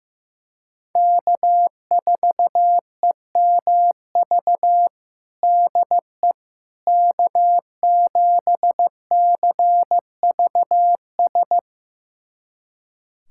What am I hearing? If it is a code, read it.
Morse code, heard as a keyed tone, about 16 wpm, 700 Hz. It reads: K4EMV DE K7CVS
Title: USA Callsign DE Callsign - 5 Characters - 15wpm